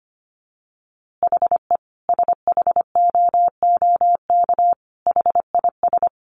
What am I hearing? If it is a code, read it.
Morse code, heard as a keyed tone, about 25 wpm, 700 Hz. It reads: HE SHOOK HIS
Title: A Sound Of Thunder 25wpm